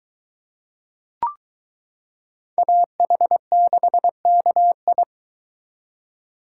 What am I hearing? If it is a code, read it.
Morse code, heard as a keyed tone, about 23 wpm, 700 Hz. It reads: AH6KI